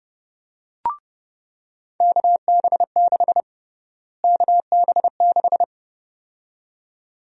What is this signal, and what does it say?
Morse code, heard as a keyed tone, about 30 wpm, 700 Hz. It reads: KB6 KB6
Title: USA Call Sign Prefixes - Repeated 2x - 30wpm